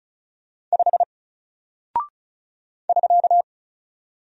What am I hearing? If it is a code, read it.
Morse code, heard as a keyed tone, about 34 wpm, 700 Hz. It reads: 5 <SK>
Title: Single Character (Letter:Number:Punctuation:Prosign) Alt 35wpm